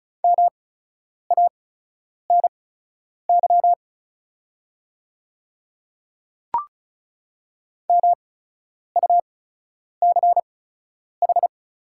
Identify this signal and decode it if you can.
Morse code, heard as a keyed tone, about 36 wpm, spaced as for 10 wpm, 690 Hz. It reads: MANY MUCH
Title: 8x Character Spacing - Top 300 Words - 35wpm